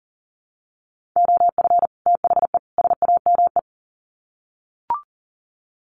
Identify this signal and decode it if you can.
Morse code, heard as a keyed tone, about 40 wpm, 700 Hz. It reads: OF THE SAME
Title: Sets of 3 Words 40wpm